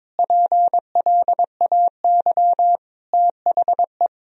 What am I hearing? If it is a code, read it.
Morse code, heard as a keyed tone, about 22 wpm, 700 Hz. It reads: PLAY THE
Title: Sentences from Top 1000 Words - Review Part 1 - 22wpm